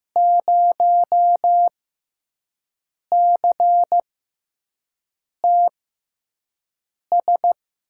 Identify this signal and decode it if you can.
Morse code, heard as a keyed tone, about 15 wpm, spaced as for 5 wpm, 700 Hz. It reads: 0CTS
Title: USA Call Signs - 6 Characters - 6x Character Spacing - 15wpm